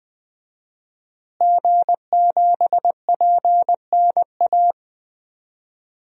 Morse code, heard as a keyed tone, about 20 wpm, 700 Hz. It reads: G7PNA
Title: DXCC-100 Call Signs - 5 Characters - Repeated 3x - 20wpm